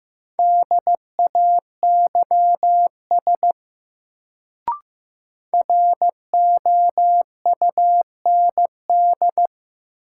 Morse code, heard as a keyed tone, about 15 wpm, 700 Hz. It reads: DAYS ROUND
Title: Sets of 5 Words Encore - New Word Drill - 15wpm